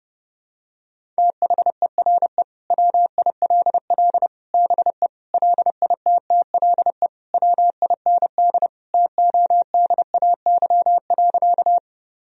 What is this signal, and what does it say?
Morse code, heard as a keyed tone, about 30 wpm, 700 Hz. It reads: THERE WILL BE LITTLE WIND TODAY.
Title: Sentences from Top 1000 Words - Review Part 2 - 30wpm